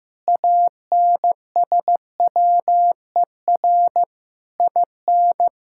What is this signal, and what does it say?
Morse code, heard as a keyed tone, about 15 wpm, 700 Hz. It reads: ANSWER IN
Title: Sentences from Top 1600 Words 15wpm